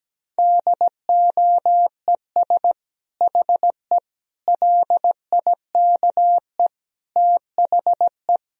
Morse code, heard as a keyed tone, about 17 wpm, 700 Hz. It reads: DOES HE LIKE THE
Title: Sentences from Top 1000 Words - Review Part 4 - 17wpm